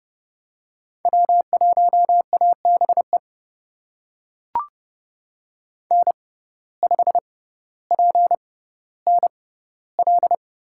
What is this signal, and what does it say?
Morse code, heard as a keyed tone, about 29 wpm, spaced as for 15 wpm, 700 Hz. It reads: W1ABE N5PNL
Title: USA Call Signs - 5 Characters - 6x Character Spacing - 30wpm